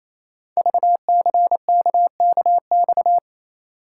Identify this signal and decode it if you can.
Morse code, heard as a keyed tone, about 28 wpm, 690 Hz. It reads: VCKKX
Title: Code Groups - 5 Letters - 28wpm